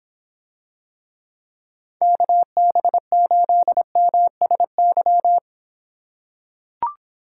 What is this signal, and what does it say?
Morse code, heard as a keyed tone, about 26 wpm, 700 Hz. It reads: KB8MSY